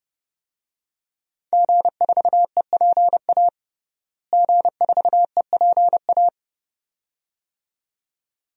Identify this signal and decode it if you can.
Morse code, heard as a keyed tone, about 30 wpm, 700 Hz. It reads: G4EPA G4EPA
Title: DXCC-100 Call Signs - 5 Characters - Repeated 2x - 30wpm